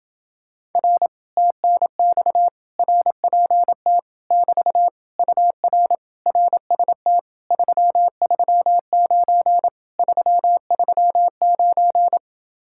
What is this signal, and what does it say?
Morse code, heard as a keyed tone, about 29 wpm, 700 Hz. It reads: R TNX RPT <BT> UR RST 339 339